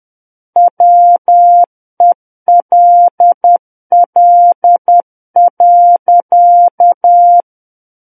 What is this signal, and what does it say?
Morse code, heard as a keyed tone, about 10 wpm, 700 Hz. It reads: WELL.